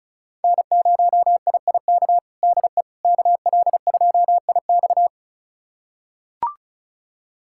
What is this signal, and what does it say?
Morse code, heard as a keyed tone, about 34 wpm, 700 Hz. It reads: N0IIK DE KL2IX
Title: USA Callsign DE Callsign - 5 Characters - 35wpm